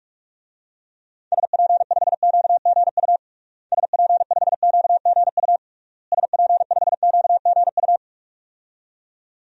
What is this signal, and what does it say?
Morse code, heard as a keyed tone, about 41 wpm, 700 Hz. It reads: SP5QGU SP5QGU SP5QGU
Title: DXCC-100 Call Signs - 6 Characters - Repeated 3x - 45wpm